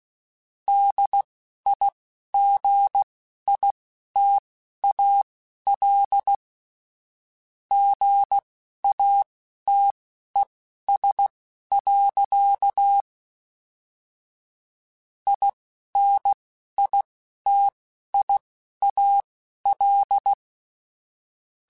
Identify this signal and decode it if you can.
Morse code, heard as a keyed tone, about 16 wpm, spaced as for 11 wpm, 800 Hz. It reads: DIGITAL GATES. INITIAL
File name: Technical Subjects / Technical Subjects-mp3s / Gates Intro @10